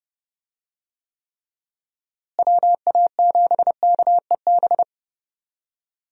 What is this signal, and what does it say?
Morse code, heard as a keyed tone, about 31 wpm, 700 Hz. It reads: WA7KEB
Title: USA Call Signs - 6 Characters - Repeated 2x - 30wpm